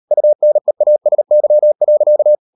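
Morse code, heard as a keyed tone, about 38 wpm, 600 Hz. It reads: UNEASY.
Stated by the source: Morse code practice at 38 WPM, sending Ken Follett's novel Never in English, from CW med Gnister.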